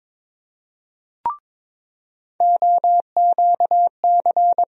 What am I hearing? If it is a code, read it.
Morse code, heard as a keyed tone, about 22 wpm, 700 Hz. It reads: OQC
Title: USA Call Sign Postfixes - 3 Letters 22wpm